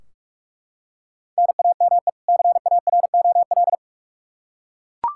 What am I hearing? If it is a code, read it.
Morse code, heard as a keyed tone, about 43 wpm, 700 Hz. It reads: NAME CAROL
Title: Practice QSO - Exchange 1 - Beginning - 45wpm